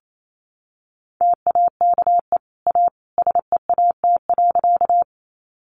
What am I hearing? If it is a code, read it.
Morse code, heard as a keyed tone, about 28 wpm, 700 Hz. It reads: TAKE A SEAT.
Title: Sentences from Top 1500 Words 28wpm